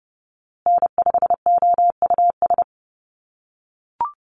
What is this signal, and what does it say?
Morse code, heard as a keyed tone, about 30 wpm, 700 Hz. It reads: N5OUS
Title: USA Call Signs Repeated 2x 30wpm